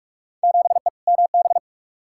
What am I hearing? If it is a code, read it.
Morse code, heard as a keyed tone, about 47 wpm, 700 Hz. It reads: 7E MB
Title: Field Day Exchanges Encore - 45wpm